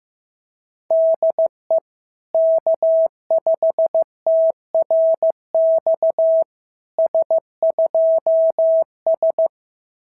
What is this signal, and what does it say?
Morse code, heard as a keyed tone, about 16 wpm, 690 Hz. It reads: DE K5TRX S2S
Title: SOTA QSO Practice 15wpm